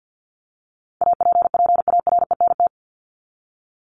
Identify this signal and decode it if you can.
Morse code, heard as a keyed tone, about 50 wpm, 700 Hz. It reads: APPARENT
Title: LY Suffix - Top 100 Words with Base Words 50wpm